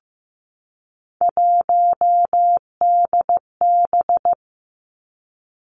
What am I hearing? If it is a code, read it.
Morse code, heard as a keyed tone, about 15 wpm, 700 Hz. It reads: 1DB
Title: Three Challenging Letter-Number Combinations 15wpm